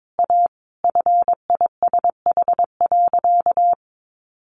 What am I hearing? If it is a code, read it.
Morse code, heard as a keyed tone, about 22 wpm, 700 Hz. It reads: A FISH.
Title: Additional Easy Phrases 22wpm